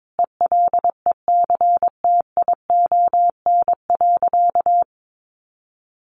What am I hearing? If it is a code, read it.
Morse code, heard as a keyed tone, about 22 wpm, 700 Hz. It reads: ELECTION.
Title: Sentences from Top 1500 Words 22wpm